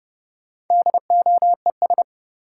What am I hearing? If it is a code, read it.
Morse code, heard as a keyed tone, about 32 wpm, 700 Hz. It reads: DOES
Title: Sets of 2 Words - New Word Drill - 30wpm